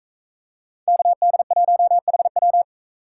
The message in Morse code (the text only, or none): KD1HW